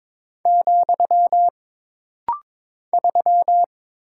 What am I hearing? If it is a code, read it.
Morse code, heard as a keyed tone, about 22 wpm, 700 Hz. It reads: , 3
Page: This rapid-fire timing is geared to push your ICR (Instant Character Recognition) skills.